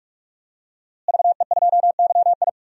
Morse code, heard as a keyed tone, about 42 wpm, 700 Hz. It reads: VE2YI
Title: DXCC-100 Call Signs - 5 Characters - Repeated 3x - 45wpm